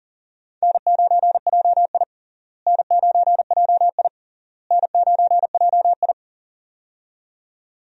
Morse code, heard as a keyed tone, about 40 wpm, 700 Hz. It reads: N9JI N9JI N9JI
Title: USA Call Signs - 4 Characters - Repeated 3x - 40wpm